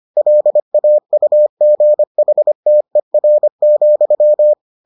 Morse code, heard as a keyed tone, about 25 wpm, 600 Hz. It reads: LAUGHTER,